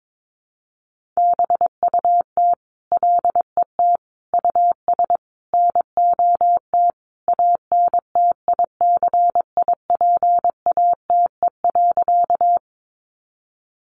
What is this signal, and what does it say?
Morse code, heard as a keyed tone, about 22 wpm, 700 Hz. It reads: BUT LET US NOT ANTICIPATE.
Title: Sentences from Top 1500 Words 22wpm